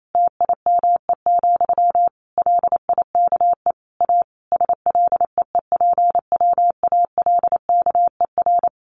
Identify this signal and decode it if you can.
Morse code, heard as a keyed tone, about 28 wpm, 700 Hz. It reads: TIME, LIKE A SLEEPWALKER
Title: A Sound Of Thunder 28wpm